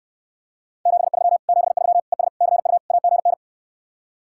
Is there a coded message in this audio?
Yes: Morse code from 0.9 to 3.3 s, heard as a keyed tone, about 68 wpm, 700 Hz. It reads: 73 73 ES CU AGN